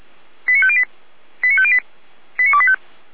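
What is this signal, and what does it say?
Five-tone selective call (ZVEI), heard as quick runs of five tones, tones of about 70 ms. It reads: 89508 89508 89285
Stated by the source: ZVEI five-tone selective call received on an SDR.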